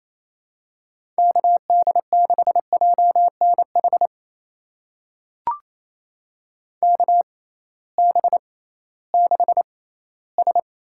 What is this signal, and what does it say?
Morse code, heard as a keyed tone, about 27 wpm, spaced as for 16 wpm, 700 Hz. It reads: KD6JNH KB6S
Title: USA Call Signs - 6 Characters - 6x Character Spacing - 28wpm